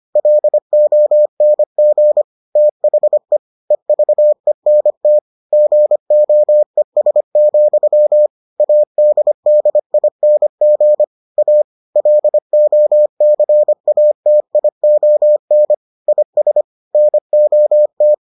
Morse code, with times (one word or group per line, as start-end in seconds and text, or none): 0.2-2.2 LONG
2.6-3.4 THE
3.7-5.2 EVENT
5.5-8.3 GOES,
8.6-11.0 ADDING
11.4-11.6 A
12.0-15.7 LOCATION
16.1-16.6 IS
16.9-18.1 NOT